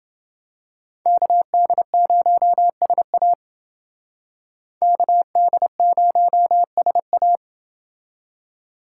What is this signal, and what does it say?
Morse code, heard as a keyed tone, about 28 wpm, 700 Hz. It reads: KD0SA KD0SA